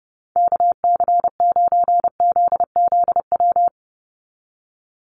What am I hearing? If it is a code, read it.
Morse code, heard as a keyed tone, about 30 wpm, 700 Hz. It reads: KC9ZZW
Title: USA Call Signs 30wpm